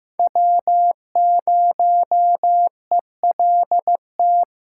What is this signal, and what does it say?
Morse code, heard as a keyed tone, about 15 wpm, 700 Hz. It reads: W0ELT
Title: USA Call Signs - 5 Characters - 4x Character Spacing - 15wpm